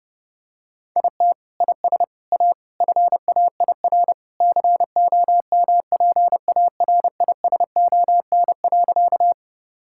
Morse code, heard as a keyed tone, about 30 wpm, 700 Hz. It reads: IT IS A FAIR COMPARISON.
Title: Sentences from Top 1600 Words 30wpm